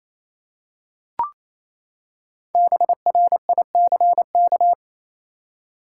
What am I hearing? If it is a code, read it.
Morse code, heard as a keyed tone, about 28 wpm, 700 Hz. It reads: BRICK
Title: Five Letter Words - 28wpm